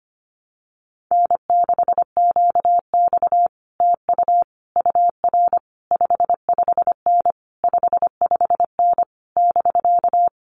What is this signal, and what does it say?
Morse code, heard as a keyed tone, about 25 wpm, 700 Hz. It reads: N6QX TU UR 55N 55N <BK>
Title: Common QSO Phrases Encore 25wpm